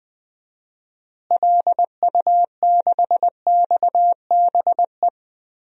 Morse code, heard as a keyed tone, about 20 wpm, 700 Hz. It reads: LU6XBE
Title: DXCC-100 Call Signs - 6 Characters - Repeated 2x - 20wpm